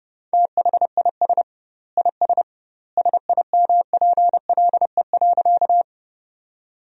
Morse code, heard as a keyed tone, about 30 wpm, 700 Hz. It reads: THIS IS SIMPLE.
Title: 2x Word-Spacing - Sentences from Top 400 Words - 30wpm